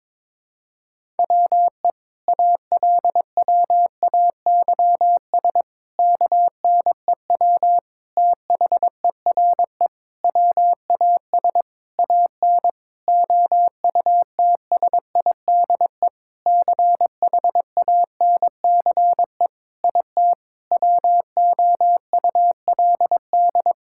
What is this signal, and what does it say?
Morse code, heard as a keyed tone, about 22 wpm, 700 Hz. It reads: WE ALWAYS KNEW THERE WAS AN OUTSIDE CHANCE IT WOULD
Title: Sentences from Top 1000 Words - Review Part 6 - 22wpm